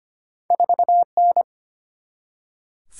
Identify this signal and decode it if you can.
Morse code, heard as a keyed tone, about 25 wpm, 700 Hz. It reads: 4N
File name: Two Letter-Number Combinations - 25wpm